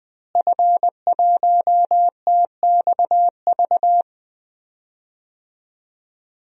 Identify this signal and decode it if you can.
Morse code, heard as a keyed tone, about 20 wpm, 700 Hz. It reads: F1TXV
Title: DXCC-100 Call Signs - 5 Characters - Repeated 2x - 20wpm